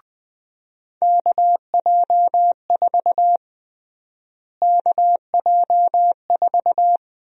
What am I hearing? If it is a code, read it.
Morse code, heard as a keyed tone, about 20 wpm, 700 Hz. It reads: KJ4 KJ4
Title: USA Call Sign Prefixes - Repeated 2x - 20wpm